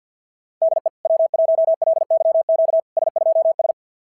Morse code, heard as a keyed tone, about 50 wpm, 660 Hz. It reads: DE W1LYQ S2S
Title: SOTA QSO Practice 50wpm